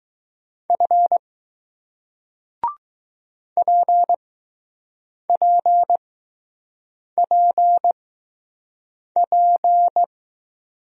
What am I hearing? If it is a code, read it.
Morse code, heard as a keyed tone, about 20 wpm, spaced as for 6 wpm, 700 Hz. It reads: F PPPP